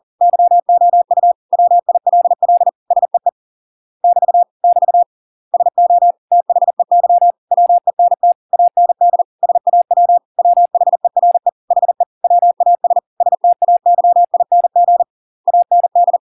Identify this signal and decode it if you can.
Morse code, heard as a keyed tone, about 40 wpm, 700 Hz. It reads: YOU WILL SEE <BT> <BT> SO THEY WENT AND SAW WHERE HE WAS STAYING AND